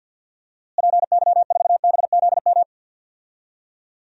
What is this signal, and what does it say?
Morse code, heard as a keyed tone, about 52 wpm, 700 Hz. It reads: PY4BZM
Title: DXCC-100 Call Signs - 6 Characters - Repeated 3x - 50wpm